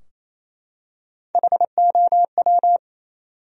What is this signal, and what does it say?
Morse code, heard as a keyed tone, about 28 wpm, 700 Hz. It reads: HOW
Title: Top 100 Words 28wpm